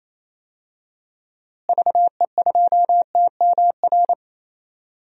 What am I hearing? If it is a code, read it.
Morse code, heard as a keyed tone, about 28 wpm, 700 Hz. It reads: VE2TMR
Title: DXCC-100 Call Signs - 6 Characters - Repeated 3x - 28wpm